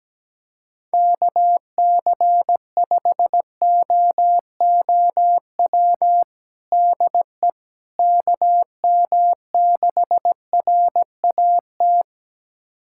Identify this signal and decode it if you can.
Morse code, heard as a keyed tone, about 17 wpm, 700 Hz. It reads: KC5OOW DE KM6RAT
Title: USA Callsign DE Callsign - 6 Characters - 17wpm